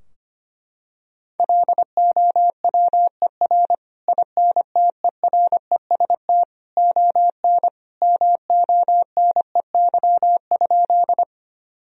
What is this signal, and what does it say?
Morse code, heard as a keyed tone, about 25 wpm, 700 Hz. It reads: LOWER INTEREST ON MONEY?
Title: Sentences from Top 1000 Words - Review Part 4 - 25wpm